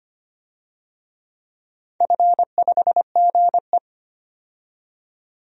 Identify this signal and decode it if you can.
Morse code, heard as a keyed tone, about 25 wpm, 690 Hz. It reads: F5GE
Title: DXCC-100 Call Signs - 4 Characters - 25wpm